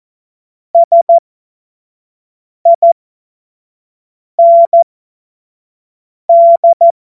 Morse code, heard as a keyed tone, about 14 wpm, spaced as for 5 wpm, 680 Hz. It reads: SIND
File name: Morsekurs / Witze / Witz 19. April 2022 25 F70